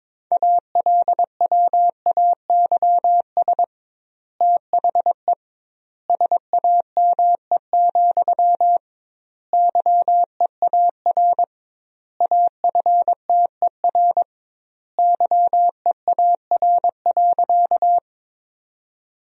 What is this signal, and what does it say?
Morse code, heard as a keyed tone, about 22 wpm, 700 Hz. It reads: ALWAYS THE SAME, YEAR AFTER YEAR.
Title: 2x Word-Spacing - Sentences from Top 300 Words - 22wpm